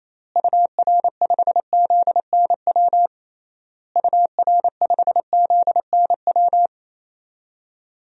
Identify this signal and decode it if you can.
Morse code, heard as a keyed tone, about 28 wpm, 700 Hz. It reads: UR5ZNW UR5ZNW
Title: DXCC-100 Call Signs - 6 Characters - Repeated 3x - 28wpm